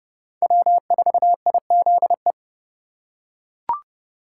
Morse code, heard as a keyed tone, about 30 wpm, 700 Hz. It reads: W4IZE
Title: USA Call Signs - 5 Characters - 3x Character Spacing - 30wpm